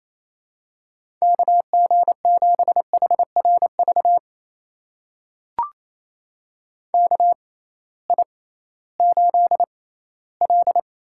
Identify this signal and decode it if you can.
Morse code, heard as a keyed tone, about 28 wpm, spaced as for 16 wpm, 700 Hz. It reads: KG7HRV KI8L
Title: USA Call Signs - 6x Character Spacing - 28wpm